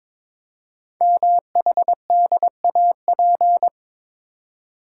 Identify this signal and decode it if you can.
Morse code, heard as a keyed tone, about 23 wpm, 700 Hz. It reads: MHDAP
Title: Code Groups - 5 Letters - 22wpm